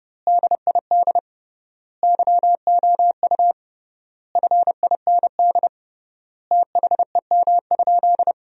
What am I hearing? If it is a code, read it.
Morse code, heard as a keyed tone, about 30 wpm, 700 Hz. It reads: DID YOU FIND THEM?
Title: 3x Word-Spacing - Sentences from Top 200 Words - 30wpm